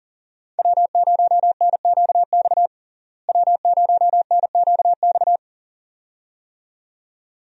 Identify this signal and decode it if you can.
Morse code, heard as a keyed tone, about 40 wpm, 700 Hz. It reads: W0NQX W0NQX
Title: USA Call Signs - 5 Characters - Repeated 2x - 40wpm